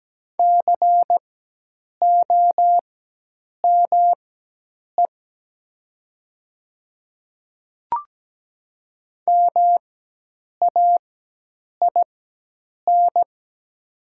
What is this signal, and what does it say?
Morse code, heard as a keyed tone, about 17 wpm, spaced as for 8 wpm, 700 Hz. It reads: COME MAIN
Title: 4x Character Spacing - Top 300 Words - 17wpm